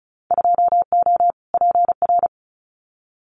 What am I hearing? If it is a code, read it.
Morse code, heard as a keyed tone, about 36 wpm, 700 Hz. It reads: 2O PR
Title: Winter Field Day Exchanges Encore 35wpm